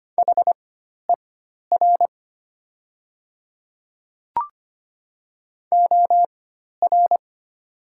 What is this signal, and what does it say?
Morse code, heard as a keyed tone, about 25 wpm, spaced as for 12 wpm, 700 Hz. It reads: HER OR
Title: 4x Character Spacing - Top 300 Words - 25wpm